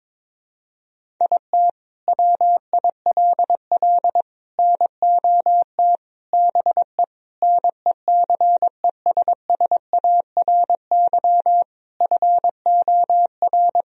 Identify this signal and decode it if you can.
Morse code, heard as a keyed tone, about 22 wpm, 700 Hz. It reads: IT WILL NOT BE NECESSARY FOR
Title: Sentences from Top 1000 Words - Review Part 1 - 22wpm